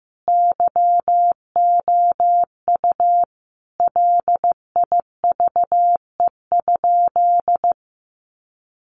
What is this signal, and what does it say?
Morse code, heard as a keyed tone, about 15 wpm, 700 Hz. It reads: YOU LIVE?
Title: Sentences from Top 200 Words 15wpm